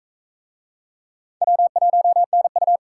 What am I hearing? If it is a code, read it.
Morse code, heard as a keyed tone, about 39 wpm, 700 Hz. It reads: W1NU